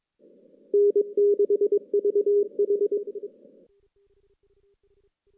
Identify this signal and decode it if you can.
Morse code, heard as a keyed tone, about 23 wpm, 410 Hz. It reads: N6VH
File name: 2016 JAN CW NAQP / 2016 Jan naqp k5ot - 0-6